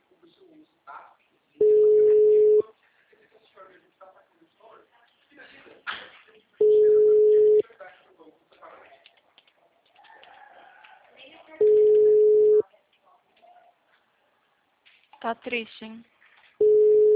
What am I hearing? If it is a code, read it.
Ringback tone (Europe, 425 Hz), with speech.